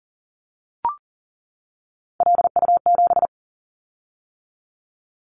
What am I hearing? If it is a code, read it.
Morse code, heard as a keyed tone, about 42 wpm, 700 Hz. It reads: LU7
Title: Callsign Prefixes for Top 50 DXCC - 40wpm